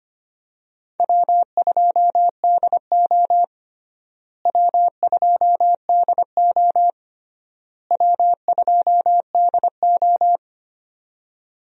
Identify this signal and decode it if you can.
Morse code, heard as a keyed tone, about 25 wpm, 700 Hz. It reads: W2DO W2DO W2DO
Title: USA Call Signs - 4 Characters - Repeated 3x - 25wpm